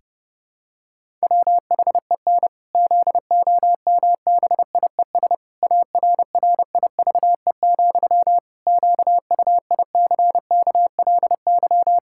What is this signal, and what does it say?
Morse code, heard as a keyed tone, about 31 wpm, 700 Hz. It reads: WHEN ZOMBIES ARRIVE, QUICKLY
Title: Pangram Challenge v2 - 30wpm